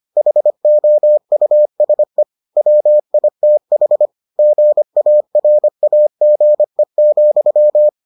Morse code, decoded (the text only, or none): HOUSE WITH GARAGE,